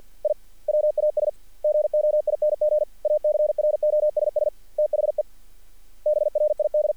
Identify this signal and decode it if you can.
Morse code, heard as a keyed tone, about 48 wpm, 600 Hz. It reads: I WAS GOING ACROSS THE BRID
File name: archived / CRSnet